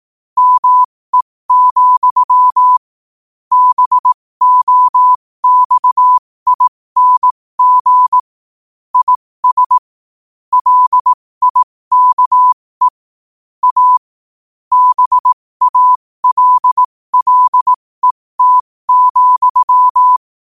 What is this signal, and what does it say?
Morse code, heard as a keyed tone, about 18 wpm, spaced as for 15 wpm, 1000 Hz. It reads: ME, BOXING IS LIKE A BALLET,
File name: qotdfiles / 15wpm